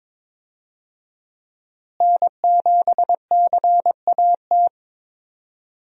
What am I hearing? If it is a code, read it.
Morse code, heard as a keyed tone, about 22 wpm, 700 Hz. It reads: N7CAT
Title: USA Call Signs - 5 Characters - Repeated 3x - 22wpm